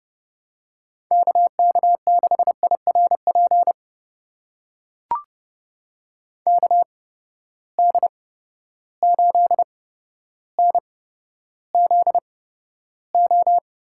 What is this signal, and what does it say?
Morse code, heard as a keyed tone, about 32 wpm, spaced as for 13 wpm, 700 Hz. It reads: KK6IRP KD8NZO
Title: USA Call Signs - 6 Characters - 8x Character Spacing - 30wpm